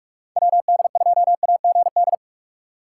Morse code, heard as a keyed tone, about 59 wpm, 700 Hz. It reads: WD2AGD